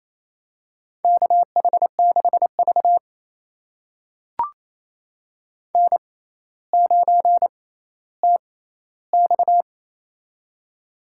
Morse code, heard as a keyed tone, about 29 wpm, spaced as for 15 wpm, 700 Hz. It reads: KH6V N9TX
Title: USA Call Signs - 4 Characters - 6x Character Spacing - 28wpm